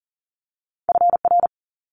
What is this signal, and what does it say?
Morse code, heard as a keyed tone, about 42 wpm, 700 Hz. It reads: FR
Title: Multi-letter Phonemes Quick Fire 40wpm